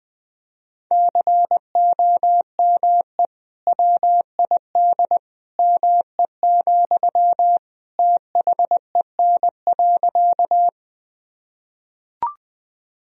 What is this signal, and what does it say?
Morse code, heard as a keyed tone, about 20 wpm, 700 Hz. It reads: COME WID ME, THEN.
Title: Sentences from Top 200 Words and Abbreviations 20wpm